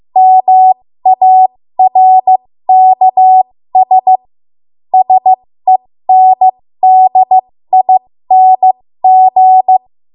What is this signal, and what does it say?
Morse code, heard as a keyed tone, about 15 wpm, spaced as for 13 wpm, 750 Hz. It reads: MARKS SENDING
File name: Morse / 13WPM